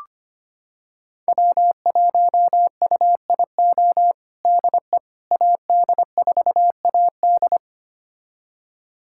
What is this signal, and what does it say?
Morse code, heard as a keyed tone, about 25 wpm, 700 Hz. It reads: W1UIO DE AD4AD
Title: USA Callsign DE Callsign - 5 Characters - 25wpm